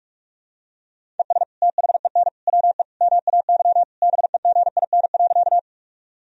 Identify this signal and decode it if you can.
Morse code, heard as a keyed tone, about 46 wpm, 700 Hz. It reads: ES THEN WE MAY BEGIN.
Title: Sentences from Top 300 Words and Abbreviations 45wpm